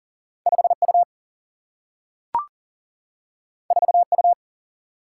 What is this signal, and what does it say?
Morse code, heard as a keyed tone, about 40 wpm, 700 Hz. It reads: 5 U 4 U
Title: Single-Letter Single-Number 40wpm